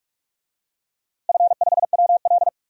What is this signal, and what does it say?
Morse code, heard as a keyed tone, about 46 wpm, 700 Hz. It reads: F5WL
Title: DXCC-100 Call Signs - 4 Characters - Repeated 2x - 45wpm